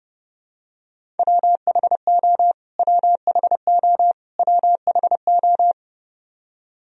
Morse code, heard as a keyed tone, about 32 wpm, 700 Hz. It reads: WHO WHO WHO
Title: Top 100 Words Repeated 30wpm